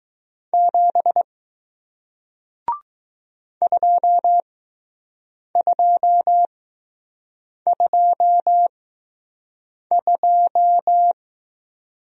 Morse code, heard as a keyed tone, about 20 wpm, spaced as for 6 wpm, 700 Hz. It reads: 7 2222